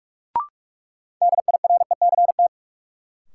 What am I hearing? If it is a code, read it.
Morse code, heard as a keyed tone, about 42 wpm, 700 Hz. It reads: DIRECT
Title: Sentences from Top 400 Words - New Word Drill - 45wpm